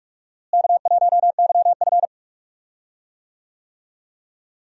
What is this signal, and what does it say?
Morse code, heard as a keyed tone, about 47 wpm, 700 Hz. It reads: K1YF
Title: USA Call Signs - 4 Characters - Repeated 3x - 45wpm